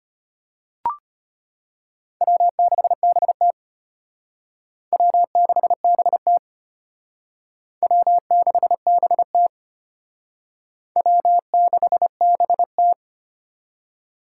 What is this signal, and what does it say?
Morse code, heard as a keyed tone, about 32 wpm, 700 Hz. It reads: W6BT W6BT W6BT W6BT